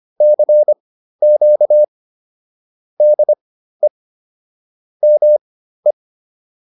Morse code, heard as a keyed tone, about 25 wpm, spaced as for 13 wpm, 600 Hz. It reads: CQ DE ME